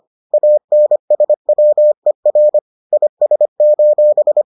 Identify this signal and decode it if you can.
Morse code, heard as a keyed tone, about 25 wpm, 600 Hz. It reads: ANSWER IS: